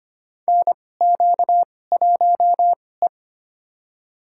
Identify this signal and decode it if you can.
Morse code, heard as a keyed tone, about 25 wpm, spaced as for 18 wpm, 700 Hz. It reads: NQ1E
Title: USA Call Signs - 4 Characters - 2x Character Spacing - 25wpm